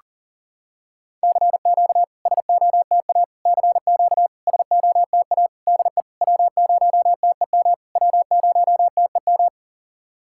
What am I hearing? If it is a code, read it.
Morse code, heard as a keyed tone, about 44 wpm, 690 Hz. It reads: CQ SOTA CQ SOTA DE W0TEM W0TEM
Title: SOTA QSO Practice 40wpm